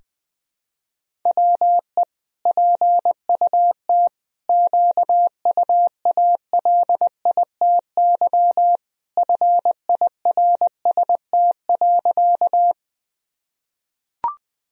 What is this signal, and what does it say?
Morse code, heard as a keyed tone, about 20 wpm, 700 Hz. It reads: WE PUT QUALITY FIRST.
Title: Sentences from Top 1000 Words - Review Part 4 - 20wpm